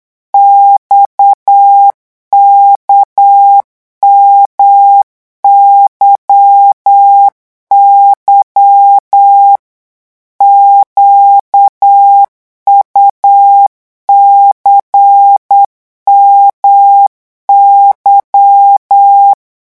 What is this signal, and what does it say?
Morse code, heard as a keyed tone, about 8 wpm, 800 Hz. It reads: XKMYY QUCMY
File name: morse mp3 / revision3-vitesse 8 mots